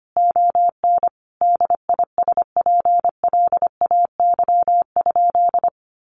Morse code, heard as a keyed tone, about 25 wpm, 700 Hz. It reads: ON DISPLAY?
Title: Sentences from Top 1200 Words 25wpm